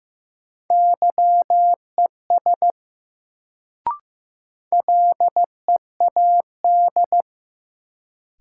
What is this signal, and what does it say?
Morse code, heard as a keyed tone, about 15 wpm, 700 Hz. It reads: YES LEAD
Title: Sentences from Top 500 Words - New Word Drill - 15wpm